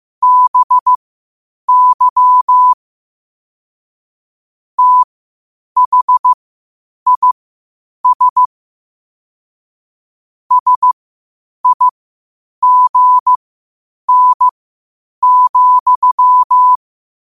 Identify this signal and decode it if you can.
Morse code, heard as a keyed tone, about 15 wpm, spaced as for 8 wpm, 1000 Hz. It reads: BY THIS SIGN,
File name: qotdfiles / 8wpm